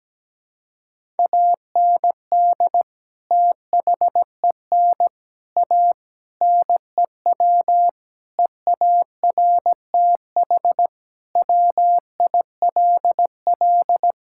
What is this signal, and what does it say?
Morse code, heard as a keyed tone, about 17 wpm, 700 Hz. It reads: AND THEN A NEW EARTH WILL
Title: Sentences from Top 1000 Words - Review Part 6 - 17wpm